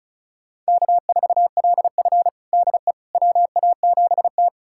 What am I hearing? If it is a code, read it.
Morse code, heard as a keyed tone, about 34 wpm, 700 Hz. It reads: K4LF DE WA7T
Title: USA Callsign DE Callsign - 4 Characters - 35wpm